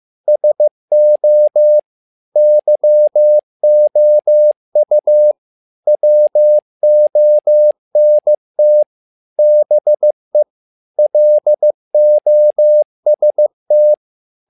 Morse code, heard as a keyed tone, about 15 wpm, 600 Hz. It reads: SO YOU WONT BE LOST
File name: learning center / tower trouble / tower trouble 15wpm